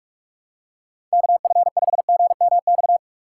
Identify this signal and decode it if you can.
Morse code, heard as a keyed tone, about 44 wpm, 700 Hz. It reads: KF5GMX
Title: USA Call Signs - 6x Character Spacing - 45wpm